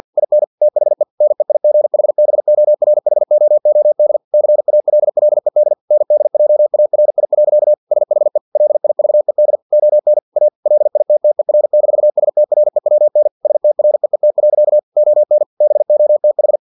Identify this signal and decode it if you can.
Morse code, heard as a keyed tone, about 50 wpm, 600 Hz. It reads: IN THE NEIGHBORHOOD CALLED NDJARI. SHE LIVED ON A LITTER-STREWN STREET. ON BOTH
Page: Never 49wpm